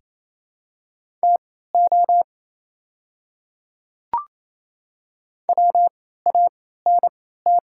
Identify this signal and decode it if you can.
Morse code, heard as a keyed tone, about 28 wpm, spaced as for 16 wpm, 700 Hz. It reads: TO WANT
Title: 3x Character Spacing - Top 200 Words - 28wpm